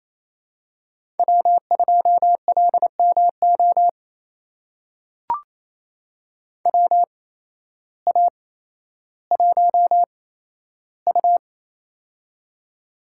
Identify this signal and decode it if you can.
Morse code, heard as a keyed tone, about 27 wpm, spaced as for 13 wpm, 700 Hz. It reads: W2LMO WA1U
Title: USA Call Signs - 8x Character Spacing - 28wpm